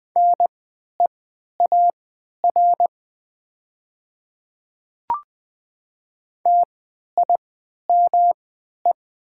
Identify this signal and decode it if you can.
Morse code, heard as a keyed tone, about 20 wpm, spaced as for 11 wpm, 690 Hz. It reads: NEAR TIME
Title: 3x Character Spacing - Top 200 Words - 20wpm